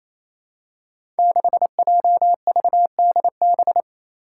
Morse code, heard as a keyed tone, about 28 wpm, 700 Hz. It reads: 6JVDB
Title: Five Challenging Letter-Number Combinations 28wpm